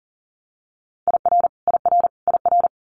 Morse code, heard as a keyed tone, about 42 wpm, 700 Hz. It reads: IR IR IR
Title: Multi-letter Phonemes Repeated 3x 40wpm